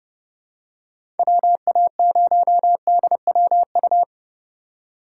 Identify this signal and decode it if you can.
Morse code, heard as a keyed tone, about 30 wpm, 700 Hz. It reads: WA0DWU
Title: USA Call Signs - 6 Characters - 4x Character Spacing - 30wpm